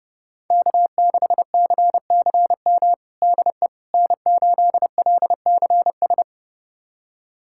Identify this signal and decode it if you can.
Morse code, heard as a keyed tone, about 30 wpm, 700 Hz. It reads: K6CCM DE N8LCS
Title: USA Callsign DE Callsign - 5 Characters - 30wpm